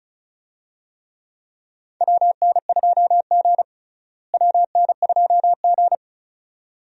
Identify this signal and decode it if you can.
Morse code, heard as a keyed tone, about 34 wpm, 700 Hz. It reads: WN2G WN2G
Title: USA Call Signs - 4 Characters - Repeated 2x - 35wpm